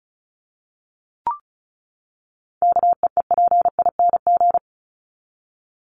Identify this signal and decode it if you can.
Morse code, heard as a keyed tone, about 34 wpm, 700 Hz. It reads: KEEPING
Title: ING Suffix - Top 100 Words with Base Words 35wpm